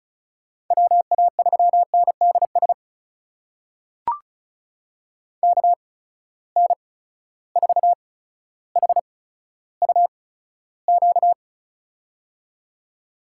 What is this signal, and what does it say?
Morse code, heard as a keyed tone, about 36 wpm, spaced as for 15 wpm, 700 Hz. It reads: WA3NDS KN4HUQ
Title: USA Call Signs - 6 Characters - 8x Character Spacing - 35wpm